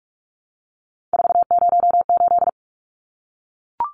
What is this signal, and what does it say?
Morse code, heard as a keyed tone, about 44 wpm, 700 Hz. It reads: 408
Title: Numbers - 3-Digit 45wpm